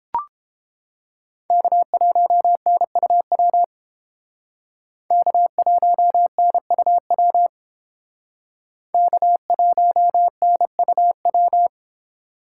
Morse code, heard as a keyed tone, about 30 wpm, 700 Hz. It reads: K1NUW K1NUW K1NUW